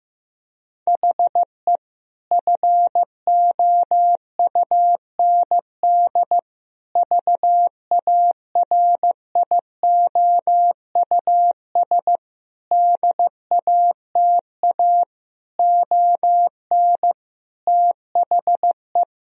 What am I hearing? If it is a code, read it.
Morse code, heard as a keyed tone, about 16 wpm, 700 Hz. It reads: HE FOUND VARIOUS DATA ON THE
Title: Sentences from Top 1000 Words - Review Part 4 - 15wpm